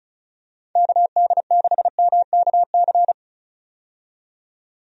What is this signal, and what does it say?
Morse code, heard as a keyed tone, about 34 wpm, 700 Hz. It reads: KD6MKC
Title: USA Call Signs - 6 Characters - 35wpm